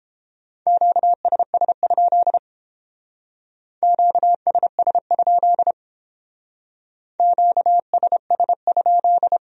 Morse code, heard as a keyed tone, about 30 wpm, 700 Hz. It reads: QSS? QSS? QSS?